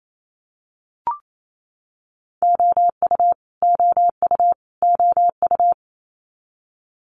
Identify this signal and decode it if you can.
Morse code, heard as a keyed tone, about 28 wpm, 700 Hz. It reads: OU OU OU
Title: Multi-letter Phonemes Repeated 3x 28wpm